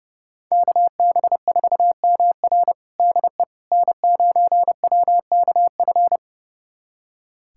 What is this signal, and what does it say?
Morse code, heard as a keyed tone, about 32 wpm, 700 Hz. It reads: KB4MR DE N9WKF
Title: USA Callsign DE Callsign - 5 Characters - 30wpm